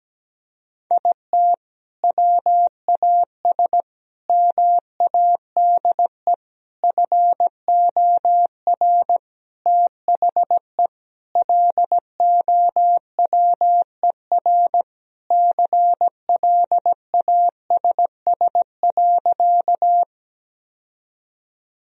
Morse code, heard as a keyed tone, about 17 wpm, 700 Hz. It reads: IT WAS MADE FOR THE LOWER CLASS.
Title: Sentences from Top 1000 Words - Review Part 4 - 17wpm